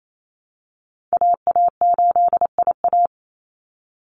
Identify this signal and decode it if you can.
Morse code, heard as a keyed tone, about 28 wpm, 700 Hz. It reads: AA8IA
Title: Call Signs Repeated 3x - 28wpm